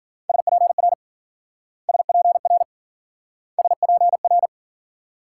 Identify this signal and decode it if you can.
Morse code, heard as a keyed tone, about 48 wpm, 700 Hz. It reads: SPR SPR SPR
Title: Speed-Racing Multi-letter Phonemes - 40wpm